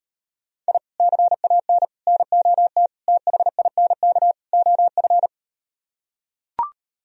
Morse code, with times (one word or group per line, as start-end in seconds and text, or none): 0.7-0.8 I
1.0-1.8 CAN
2.1-2.9 NOT
3.1-4.3 THINK
4.5-5.3 OF